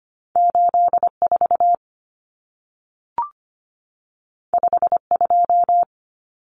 Morse code, heard as a keyed tone, about 25 wpm, 700 Hz. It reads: 84 52
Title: Numbers - 2-Digit 25wpm